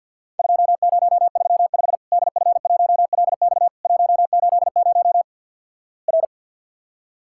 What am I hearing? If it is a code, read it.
Morse code, heard as a keyed tone, about 54 wpm, 700 Hz. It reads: 2035 DF1LX 180 R
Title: Worked All Europe DX Contest - QTC Exchange with Repeats - 50wpm